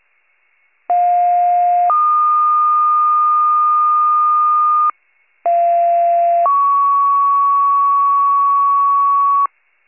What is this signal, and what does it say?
Two-tone page, heard as a one-second tone, then a three-second tone, about 710 then 1220 Hz, about 690 then 1120 Hz. It is heard from 0.9 to 9.5 s.